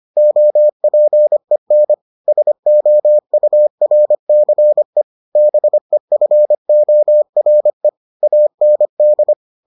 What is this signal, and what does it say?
Morse code, heard as a keyed tone, about 25 wpm, 600 Hz. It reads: OPEN SOURCE BEFORE AND